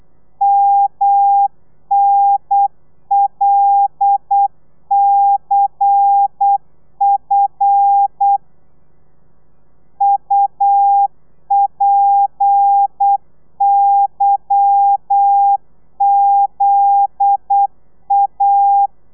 Morse code, heard as a keyed tone, about 8 wpm, 800 Hz. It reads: MNLCF UPYZA
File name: telegrafia / i2tfj / Lez11 35